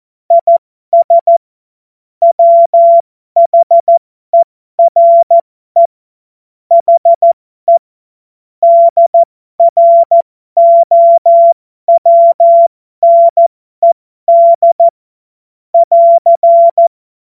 Morse code, heard as a keyed tone, about 14 wpm, spaced as for 12 wpm, 680 Hz. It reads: IS WHERE HE DROWNED <AR>